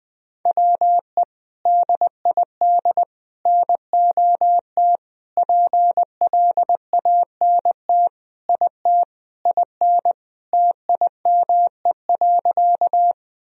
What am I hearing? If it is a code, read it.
Morse code, heard as a keyed tone, about 20 wpm, 700 Hz. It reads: WE DID NOT PLANT IT IN TIME.
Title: Sentences from Top 1000 Words - Review Part 1 - 20wpm